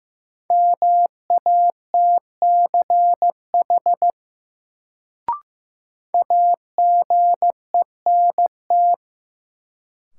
Morse code, heard as a keyed tone, about 15 wpm, 700 Hz. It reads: MATCH AGENT
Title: Sentences from Top 1600 Words - New Word Drill Alt - 15wpm